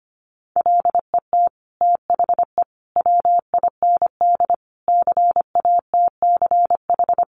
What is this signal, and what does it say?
Morse code, heard as a keyed tone, about 25 wpm, 700 Hz. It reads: LET THE WIND CATCH
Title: Sentences from Top 1500 Words 25wpm